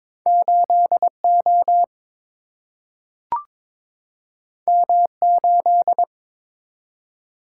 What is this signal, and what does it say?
Morse code, heard as a keyed tone, about 22 wpm, 700 Hz. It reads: 8O M8
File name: practice / 22wpm / Two Letter-Number Combinations - 22wpm